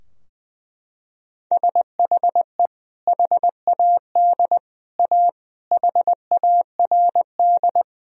Morse code, heard as a keyed tone, about 20 wpm, 700 Hz. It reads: SHE HAD A HARD
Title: Sentences from Top 1000 Words - Review Part 2 - 20wpm